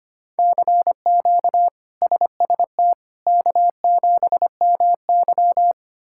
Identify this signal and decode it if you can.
Morse code, heard as a keyed tone, about 25 wpm, 700 Hz. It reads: CQ SST K7MY
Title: K1USN SST QSO Practice - 25wpm